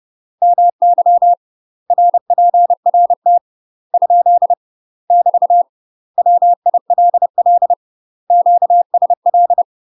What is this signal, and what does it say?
Morse code, heard as a keyed tone, about 30 wpm, 700 Hz. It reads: MY RPRT ? <BT> WILL QSL
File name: UK MASTER QSOs TRAINING / 30WPM / UK MASTER QSOs 0022